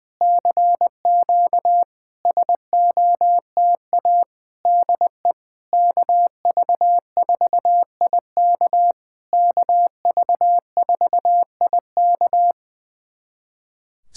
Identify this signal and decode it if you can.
Morse code, heard as a keyed tone, about 20 wpm, 700 Hz. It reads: CQ SOTA DE KV4IK KV4IK
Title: SOTA QSO Practice 20wpm